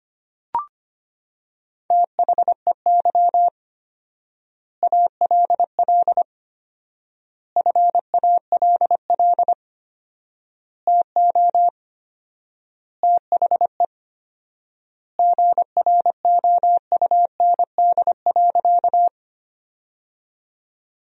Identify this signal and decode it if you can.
Morse code, heard as a keyed tone, about 25 wpm, 700 Hz. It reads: THEY ALL FALL TO THE GROUND.
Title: 4x Word-Spacing - Sentences from Top 500 Words - 25wpm